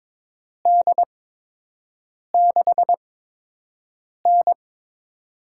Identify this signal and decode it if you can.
Morse code, heard as a keyed tone, about 22 wpm, spaced as for 6 wpm, 700 Hz. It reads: D6N
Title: USA Call Signs - 4 Characters - 8x Character Spacing - 22wpm